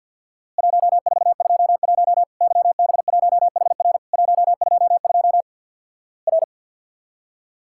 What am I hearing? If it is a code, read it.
Morse code, heard as a keyed tone, about 50 wpm, 700 Hz. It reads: 1421 YB1HR 122 R
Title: Worked All Europe DX Contest - QTC Exchange with Repeats - 50wpm